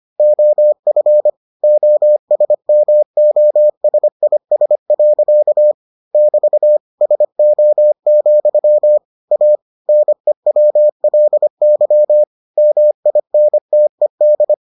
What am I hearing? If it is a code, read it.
Morse code, heard as a keyed tone, about 25 wpm, 600 Hz. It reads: OF OSMOSIS. <BT> SO, A NEWLY MINTED